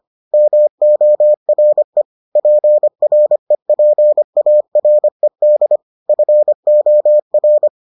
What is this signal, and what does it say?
Morse code, heard as a keyed tone, about 25 wpm, 600 Hz. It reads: MORE PREPARED FOR